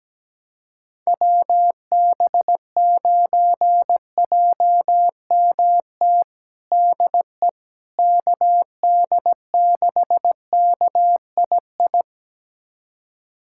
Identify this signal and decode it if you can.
Morse code, heard as a keyed tone, about 17 wpm, 700 Hz. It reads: WB9JMT DE KD6KII
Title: USA Callsign DE Callsign - 6 Characters - 17wpm